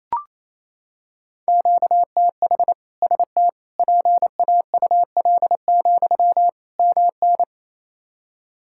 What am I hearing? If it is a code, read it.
Morse code, heard as a keyed tone, about 28 wpm, 700 Hz. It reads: QTH ST PAUL, MN
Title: K1USN SST QSO Practice 28wpm